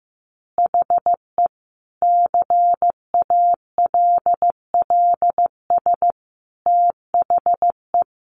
Morse code, heard as a keyed tone, about 15 wpm, 700 Hz. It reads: HE CALLS THE
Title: Additional Easy Phrases 15wpm